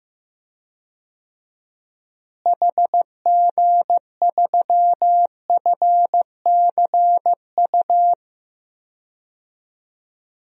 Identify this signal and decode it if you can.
Morse code, heard as a keyed tone, about 15 wpm, 700 Hz. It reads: HG3FCU
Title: DXCC-100 Call Signs - 6 Characters - Repeated 3x - 15wpm